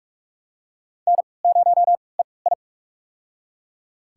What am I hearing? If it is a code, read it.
Morse code, heard as a keyed tone, about 45 wpm, spaced as for 25 wpm, 700 Hz. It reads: N0EI
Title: USA Call Signs - 4 Characters - 3x Character Spacing - 45wpm